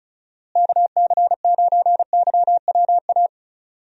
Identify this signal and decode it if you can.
Morse code, heard as a keyed tone, about 36 wpm, 700 Hz. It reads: KC9YWA
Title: USA Call Signs - 6 Characters - Repeated 2x - 35wpm